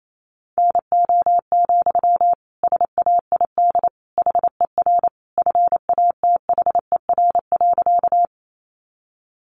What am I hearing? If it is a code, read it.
Morse code, heard as a keyed tone, about 28 wpm, 700 Hz. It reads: NO, SAID HER FATHER.
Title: Sentences from Top 300 Words 28wpm